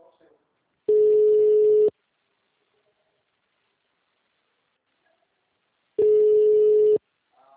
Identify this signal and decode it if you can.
Ringback tone (Europe, 425 Hz), with speech.